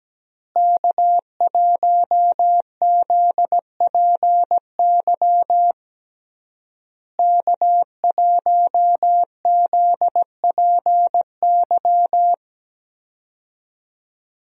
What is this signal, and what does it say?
Morse code, heard as a keyed tone, about 17 wpm, 700 Hz. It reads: K1ZPY K1ZPY
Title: USA Call Signs - 5 Characters - Repeated 3x - 17wpm